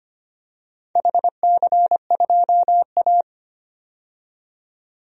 Morse code, heard as a keyed tone, about 25 wpm, 700 Hz. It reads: HC2A
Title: DXCC-100 Call Signs - 4 Characters - 25wpm